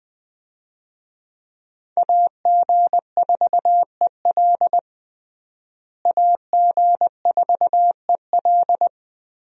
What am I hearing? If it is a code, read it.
Morse code, heard as a keyed tone, about 20 wpm, 700 Hz. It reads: AG4EL AG4EL
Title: USA Call Signs - 5 Characters - Repeated 3x - 20wpm